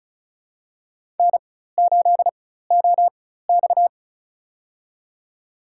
Morse code, heard as a keyed tone, about 36 wpm, spaced as for 16 wpm, 700 Hz. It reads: N8OX
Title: USA Call Signs - 4 Characters - 4x Character Spacing - 35wpm